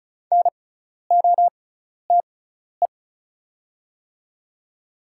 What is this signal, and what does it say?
Morse code, heard as a keyed tone, about 34 wpm, spaced as for 12 wpm, 690 Hz. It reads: NOTE
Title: 6x Character Spacing - Top 500 Words - 35wpm